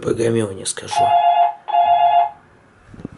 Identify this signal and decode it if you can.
Speech. There is no code.